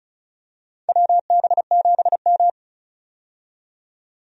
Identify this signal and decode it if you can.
Morse code, heard as a keyed tone, about 36 wpm, 690 Hz. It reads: WB7M
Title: USA Call Signs - 4 Characters - 35wpm